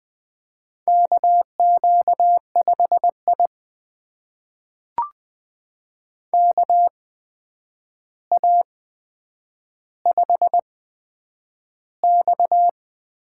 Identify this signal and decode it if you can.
Morse code, heard as a keyed tone, about 20 wpm, spaced as for 9 wpm, 700 Hz. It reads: KQ5I KA5X
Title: USA Call Signs - 4 Characters - 8x Character Spacing - 20wpm